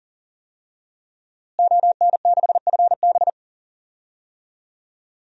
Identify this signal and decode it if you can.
Morse code, heard as a keyed tone, about 40 wpm, 700 Hz. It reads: ON6FB
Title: DXCC-100 Call Signs - 5 Characters - Repeated 3x - 40wpm